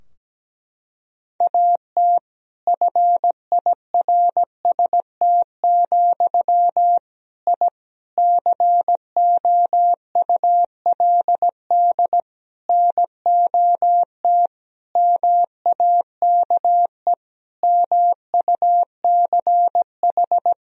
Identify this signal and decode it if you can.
Morse code, heard as a keyed tone, about 17 wpm, 700 Hz. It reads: AT FIRST, I COULD NOT MAKE MUCH
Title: Sentences from Top 1000 Words - Review Part 4 - 17wpm